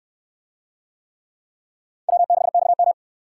Morse code, heard as a keyed tone, about 68 wpm, 700 Hz. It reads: W6XM